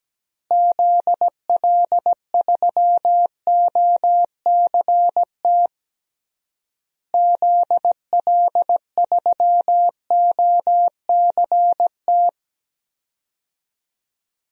Morse code, heard as a keyed tone, about 17 wpm, 700 Hz. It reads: ZL3OCT ZL3OCT
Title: DXCC-100 Call Signs - 6 Characters - Repeated 3x - 17wpm